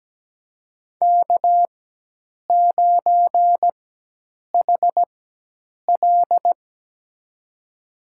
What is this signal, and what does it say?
Morse code, heard as a keyed tone, about 17 wpm, spaced as for 8 wpm, 700 Hz. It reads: K9HL
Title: USA Call Signs - 4 Characters - 4x Character Spacing - 17wpm